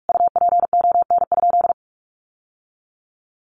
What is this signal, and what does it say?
Morse code, heard as a keyed tone, about 44 wpm, 700 Hz. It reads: UPON?
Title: Sentences from Top 1100 Words 45wpm